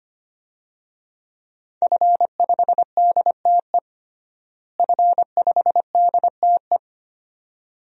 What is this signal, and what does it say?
Morse code, heard as a keyed tone, about 25 wpm, 700 Hz. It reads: F5DTE F5DTE
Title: DXCC-100 Call Signs - 5 Characters - Repeated 2x - 25wpm